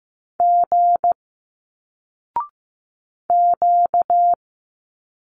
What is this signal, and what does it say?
Morse code, heard as a keyed tone, about 15 wpm, 700 Hz. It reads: G Q
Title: Single Letters 15wpm